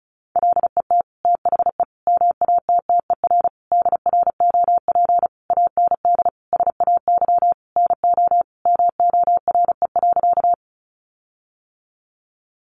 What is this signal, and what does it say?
Morse code, heard as a keyed tone, about 34 wpm, 700 Hz. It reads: LET THE MATTER DROP AND SAY NO MORE.
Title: Sentences from Top 1400 Words 35wpm